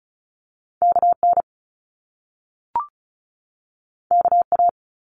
Morse code, heard as a keyed tone, about 36 wpm, 700 Hz. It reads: KN KA
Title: Top Two Letter Combinations in Words 35wpm